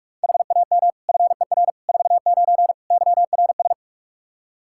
Morse code, heard as a keyed tone, about 42 wpm, 700 Hz. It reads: HAM FER 49 YRS
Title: K1USN SST QSO Practice 45wpm